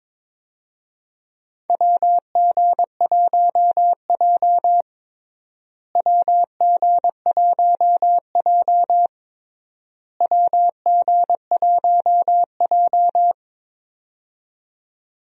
Morse code, heard as a keyed tone, about 22 wpm, 700 Hz. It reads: WG1J WG1J WG1J
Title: USA Call Signs - 4 Characters - Repeated 3x - 22wpm